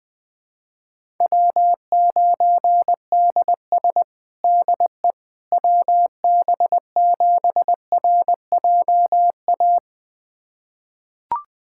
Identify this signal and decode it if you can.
Morse code, heard as a keyed tone, about 20 wpm, 700 Hz. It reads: W9DS DE WB7RJA
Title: USA Callsign DE Callsign 20wpm